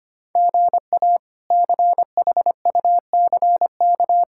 Morse code, heard as a keyed tone, about 25 wpm, 700 Hz. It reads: GA CHUCK
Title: K1USN SST QSO Practice - 25wpm